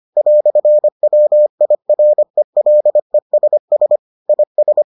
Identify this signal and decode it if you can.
Morse code, heard as a keyed tone, about 25 wpm, 600 Hz. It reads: "WIRELESS IS